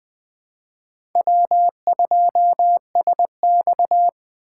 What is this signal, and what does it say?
Morse code, heard as a keyed tone, about 20 wpm, 700 Hz. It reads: W2SX
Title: USA Call Signs - 4 Characters - 2x Character Spacing - 20wpm